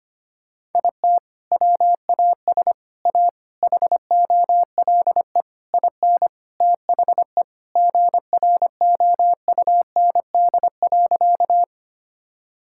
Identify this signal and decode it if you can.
Morse code, heard as a keyed tone, about 25 wpm, 700 Hz. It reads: IT WAS A HOLE IN THE GROUND.
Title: Sentences from Top 1600 Words 25wpm